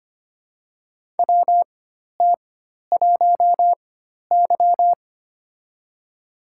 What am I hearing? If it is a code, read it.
Morse code, heard as a keyed tone, about 25 wpm, spaced as for 12 wpm, 700 Hz. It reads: WT1Y
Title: USA Call Signs - 4 Characters - 4x Character Spacing - 25wpm